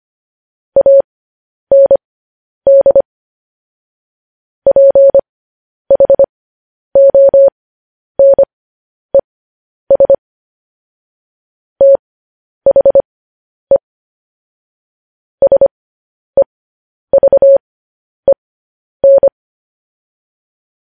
Morse code, heard as a keyed tone, about 25 wpm, spaced as for 10 wpm, 550 Hz. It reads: AND PHONES THE SEVEN
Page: news-headlines-in-morse-code-at-10-wpm